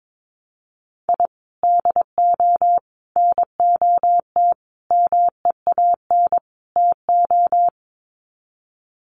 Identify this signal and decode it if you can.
Morse code, heard as a keyed tone, about 22 wpm, 700 Hz. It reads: I DO NOT MEAN TO
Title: Sets of 5 Words 22wpm